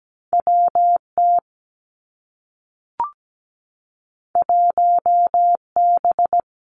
Morse code, heard as a keyed tone, about 17 wpm, 700 Hz. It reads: W T 1 B
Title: Two Challenging Letter-Number Combinations 17wpm